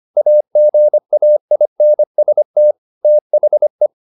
Morse code, heard as a keyed tone, about 25 wpm, 600 Hz. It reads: AGAINST THE
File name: the old man and the sea part 1 25wpm